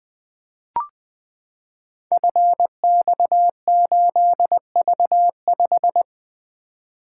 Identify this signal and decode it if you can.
Morse code, heard as a keyed tone, about 20 wpm, 700 Hz. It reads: FX8V5
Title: Code Groups - 5 Letter:Numbers - 20wpm